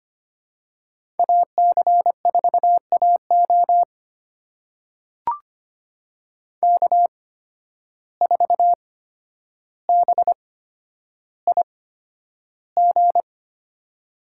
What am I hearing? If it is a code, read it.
Morse code, heard as a keyed tone, about 25 wpm, spaced as for 11 wpm, 700 Hz. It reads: AC4AO K4BIG
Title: USA Call Signs - 5 Characters - 8x Character Spacing - 25wpm